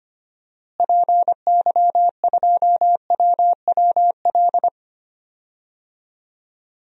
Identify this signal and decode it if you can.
Morse code, heard as a keyed tone, about 25 wpm, 700 Hz. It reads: PY2WWL
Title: DXCC-100 Call Signs - 6 Characters - Repeated 2x - 25wpm